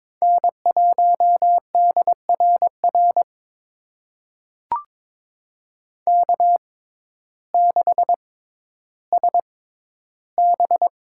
Morse code, heard as a keyed tone, about 22 wpm, spaced as for 12 wpm, 700 Hz. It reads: N1DRR K6SB
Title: USA Call Signs - 6x Character Spacing - 22wpm